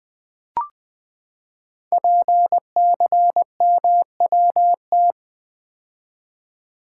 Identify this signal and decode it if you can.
Morse code, heard as a keyed tone, about 20 wpm, 700 Hz. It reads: PCMWT
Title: Code Groups - 5 Letters - 20wpm